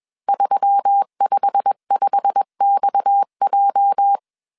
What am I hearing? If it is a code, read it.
Morse code, heard as a keyed tone, about 21 wpm, 800 Hz. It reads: 355XJ
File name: CW / DET / 1 9 1500C S C 20wpm 2310